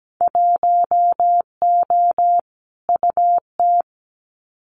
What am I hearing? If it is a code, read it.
Morse code, heard as a keyed tone, about 17 wpm, 700 Hz. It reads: 1O UT
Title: Winter Field Day Exchanges Encore 17wpm